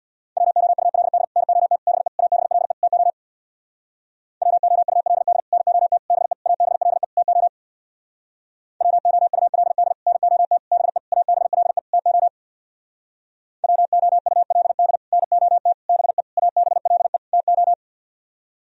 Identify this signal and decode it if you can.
Morse code, heard as a keyed tone, about 64 wpm, 690 Hz. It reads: WOULD NOT BE ABLE TO WOULD NOT BE ABLE TO WOULD NOT BE ABLE TO WOULD NOT BE ABLE TO